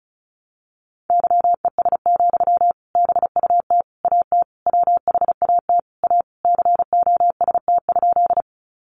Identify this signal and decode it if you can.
Morse code, heard as a keyed tone, about 34 wpm, 700 Hz. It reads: YES, BUT AT WHAT A COST?
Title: Sentences from Top 800 Words 35wpm